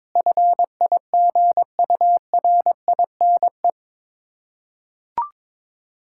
Morse code, heard as a keyed tone, about 22 wpm, 690 Hz. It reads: FIGURINE
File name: Difficult Words with Hints - 22wpm